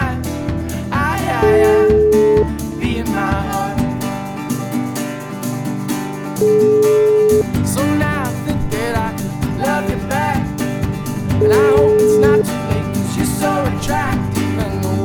Ringback tone (Europe, 425 Hz), with noise.